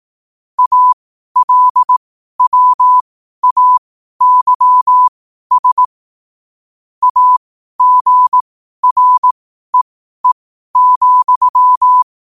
Morse code, heard as a keyed tone, about 18 wpm, spaced as for 12 wpm, 1000 Hz. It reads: ALWAYS AGREE,
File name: qotdfiles / 12wpm